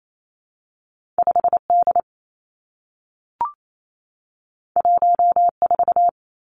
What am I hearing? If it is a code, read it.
Morse code, heard as a keyed tone, about 28 wpm, 700 Hz. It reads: 5 D 1 4
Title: Two Challenging Letter-Number Combinations 28wpm